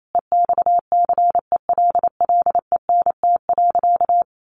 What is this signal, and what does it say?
Morse code, heard as a keyed tone, about 28 wpm, 700 Hz. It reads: EXCELLENT.
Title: Sentences from Top 1000 Words 28wpm